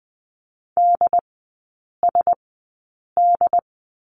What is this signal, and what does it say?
Morse code, heard as a keyed tone, about 20 wpm, spaced as for 8 wpm, 700 Hz. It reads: DSD
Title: 100 Lesson 17e - Letter u - Three Characters 20wpm